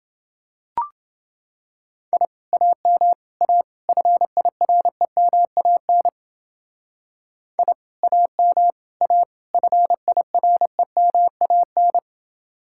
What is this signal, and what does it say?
Morse code, heard as a keyed tone, about 29 wpm, 700 Hz. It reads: I AM A FIREMAN I AM A FIREMAN